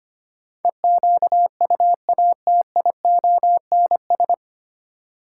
Morse code, heard as a keyed tone, about 25 wpm, 700 Hz. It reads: EQUATIONS
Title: Sets of 5 Words - New Word Drill - 25wpm